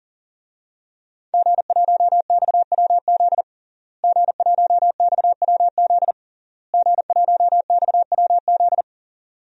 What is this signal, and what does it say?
Morse code, heard as a keyed tone, about 40 wpm, 700 Hz. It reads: G1XWZ G1XWZ G1XWZ
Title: DXCC-100 Call Signs - 5 Characters - Repeated 3x - 40wpm